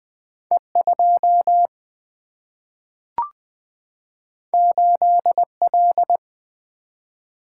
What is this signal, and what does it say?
Morse code, heard as a keyed tone, about 20 wpm, 700 Hz. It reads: E2 8L
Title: Single-Letter Single-Number 20wpm